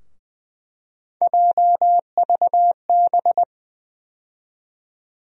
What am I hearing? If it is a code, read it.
Morse code, heard as a keyed tone, about 20 wpm, 700 Hz. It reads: JVB
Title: USA Call Sign Postfixes - 3 Letters 20wpm